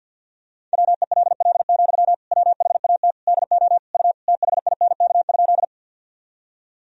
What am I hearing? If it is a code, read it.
Morse code, heard as a keyed tone, about 47 wpm, 700 Hz. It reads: WELL, WHAT DO U THINK?
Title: Sentences from Top 200 Words and Abbreviations 50wpm